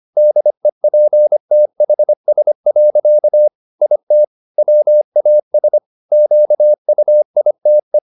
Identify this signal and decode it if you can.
Morse code, heard as a keyed tone, about 25 wpm, 600 Hz. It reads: DEPTHS. IT WAS QUITE
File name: the old man and the sea part 1 25wpm